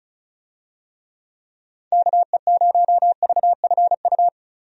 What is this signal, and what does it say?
Morse code, heard as a keyed tone, about 36 wpm, 700 Hz. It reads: KE0VFU
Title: USA Call Signs - 6 Characters - Repeated 2x - 35wpm